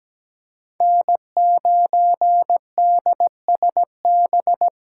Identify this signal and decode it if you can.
Morse code, heard as a keyed tone, about 17 wpm, 700 Hz. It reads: N9DSB
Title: USA Call Signs - 5 Characters - 3x Character Spacing - 17wpm